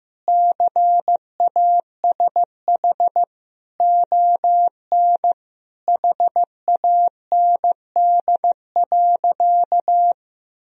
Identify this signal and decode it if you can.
Morse code, heard as a keyed tone, about 15 wpm, 700 Hz. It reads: CASH ON HAND.
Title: Sentences from Top 1000 Words - Review Part 4 - 15wpm